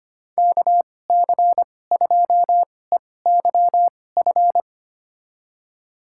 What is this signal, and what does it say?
Morse code, heard as a keyed tone, about 25 wpm, spaced as for 18 wpm, 700 Hz. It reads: KC2EYF
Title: USA Call Signs - 6 Characters - 2x Character Spacing - 25wpm